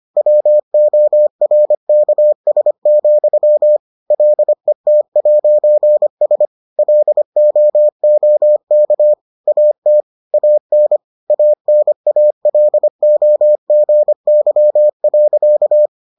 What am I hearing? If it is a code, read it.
Morse code, heard as a keyed tone, about 25 wpm, 600 Hz. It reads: WORKS, LET'S LOOK AT AN ANALOGY.